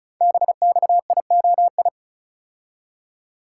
Morse code, heard as a keyed tone, about 36 wpm, 700 Hz. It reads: BXIOI
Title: Code Groups - 5 Letters - 35wpm